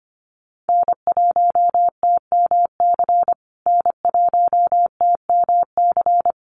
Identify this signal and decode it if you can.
Morse code, heard as a keyed tone, about 25 wpm, 700 Hz. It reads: N1TMC N1TMC
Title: USA Call Signs Repeated 2x 25wpm